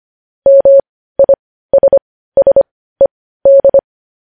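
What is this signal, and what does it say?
Morse code, heard as a keyed tone, about 25 wpm, spaced as for 15 wpm, 550 Hz. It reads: MISSED
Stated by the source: Morse code news headlines at 15 wpm.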